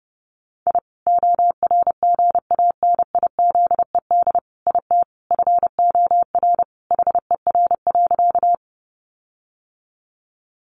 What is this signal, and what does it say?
Morse code, heard as a keyed tone, about 30 wpm, 700 Hz. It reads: I ORGANIZED IT FOR HER.
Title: Sentences from Top 1300 Words 30wpm